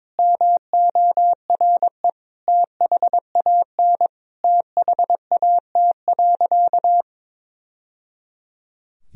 Morse code, heard as a keyed tone, about 22 wpm, 700 Hz. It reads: MORE THAN THAT.
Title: Sentences from Top 1000 Words - Review Part 3 - 22wpm